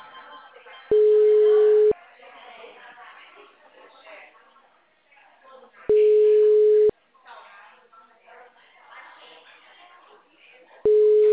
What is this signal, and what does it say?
Ringback tone (Europe, 425 Hz), with speech.